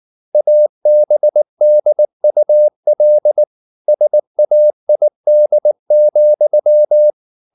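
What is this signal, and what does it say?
Morse code, heard as a keyed tone, about 19 wpm, 600 Hz. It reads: ABDUL SAID,
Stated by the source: Morse code at 19 wpm from CW med Gnister, reading Ken Follett's novel Never in English.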